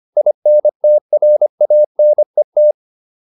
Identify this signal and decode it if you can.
Morse code, heard as a keyed tone, about 25 wpm, 600 Hz. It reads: INTRANET